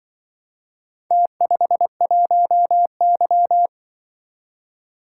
Morse code, heard as a keyed tone, about 25 wpm, 700 Hz. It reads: T51Y
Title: Speed-Racing - Four Letter-Number Combinations - 20wpm